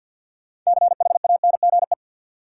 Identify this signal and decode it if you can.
Morse code, heard as a keyed tone, about 53 wpm, 700 Hz. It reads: CHANGE
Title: Sentences from Top 600 Words - New Word Drill - 50wpm